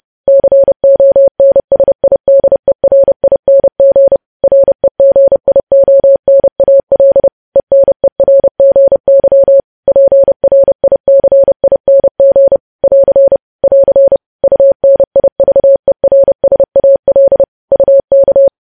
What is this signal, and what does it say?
Morse code, heard as a keyed tone, about 30 wpm, 550 Hz. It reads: CONSIDERING REGIONAL ENERGY PRICING <AR> <AR> UNIVERSAL UK
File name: podcast / News headlines-1x-30x30